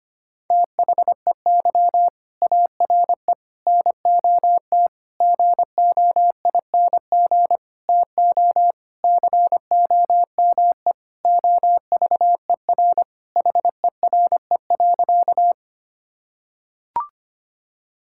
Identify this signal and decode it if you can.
Morse code, heard as a keyed tone, about 25 wpm, 700 Hz. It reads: THEY ARE NOT GOING TO COME OVER HERE.
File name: practice / 25wpm / Sentences with 4-Word Phrases - 25wpm